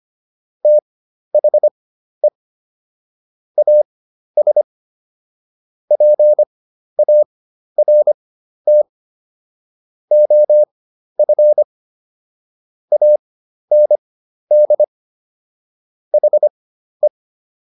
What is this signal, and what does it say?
Morse code, heard as a keyed tone, about 25 wpm, spaced as for 12 wpm, 600 Hz. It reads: THE AS PART OF AND HE